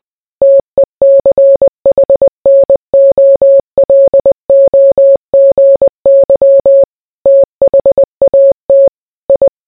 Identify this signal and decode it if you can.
Morse code, heard as a keyed tone, about 20 wpm, 550 Hz. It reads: TECHNOLOGY THAT I